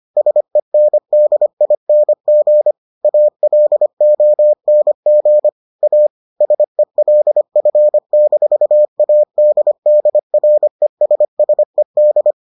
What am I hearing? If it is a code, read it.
Morse code, heard as a keyed tone, about 25 wpm, 600 Hz. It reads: SENDING ALONG A SELF-ADDRESSED